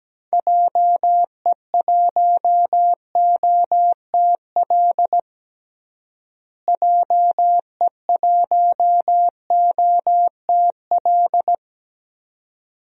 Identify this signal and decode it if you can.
Morse code, heard as a keyed tone, about 17 wpm, 700 Hz. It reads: JE1OTL JE1OTL
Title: DXCC-100 Call Signs - 6 Characters - Repeated 2x - 17wpm